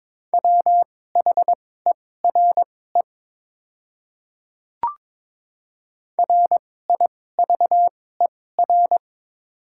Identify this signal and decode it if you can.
Morse code, heard as a keyed tone, about 22 wpm, spaced as for 16 wpm, 700 Hz. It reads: WHERE RIVER
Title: 2x Character Spacing - Top 300 Words - 22wpm